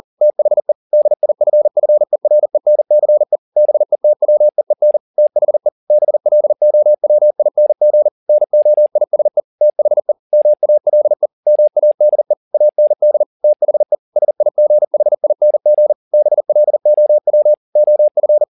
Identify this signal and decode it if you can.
Morse code, heard as a keyed tone, about 40 wpm, 600 Hz. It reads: THE DIFFERENCE BETWEEN THE BLOWING NOISE THE MALE MADE AND THE SIGHING BLOW OF